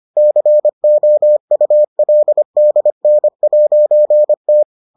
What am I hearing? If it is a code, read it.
Morse code, heard as a keyed tone, about 25 wpm, 600 Hz. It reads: COULDN'T